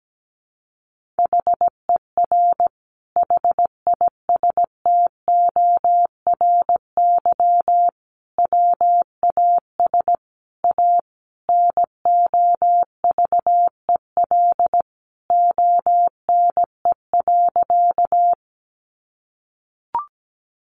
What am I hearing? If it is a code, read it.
Morse code, heard as a keyed tone, about 17 wpm, 700 Hz. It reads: HER HISTORY WAS A NOVEL ONE.
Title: Sentences from Top 1300 Words 17wpm